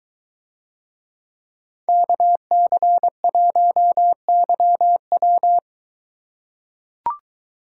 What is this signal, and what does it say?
Morse code, heard as a keyed tone, about 23 wpm, 700 Hz. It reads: KC1YW